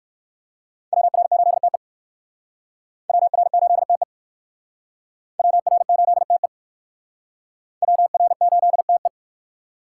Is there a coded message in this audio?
Yes: Morse code from 0.9 to 9.1 s, heard as a keyed tone, about 56 wpm, 700 Hz. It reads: WR8TE WR8TE WR8TE WR8TE